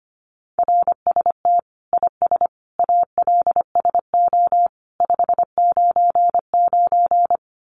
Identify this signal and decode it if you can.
Morse code, heard as a keyed tone, about 25 wpm, 700 Hz. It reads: RST IS ALSO 599
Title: Common QSO Phrases 25wpm